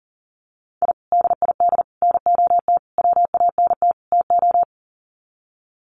Morse code, heard as a keyed tone, about 38 wpm, 700 Hz. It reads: I DID NOT WANT TO
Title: Sets of 5 Words 40wpm